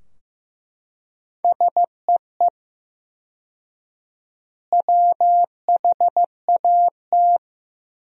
Morse code, heard as a keyed tone, about 15 wpm, 700 Hz. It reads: SEE WHAT
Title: 4x Word-Spacing - Sentences from Top 500 Words - 15wpm